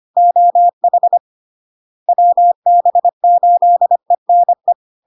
Morse code, heard as a keyed tone, about 25 wpm, 700 Hz. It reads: OH WB8ENE
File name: LARK-FD-2022-Log-in-CW-25wpm-700-Hz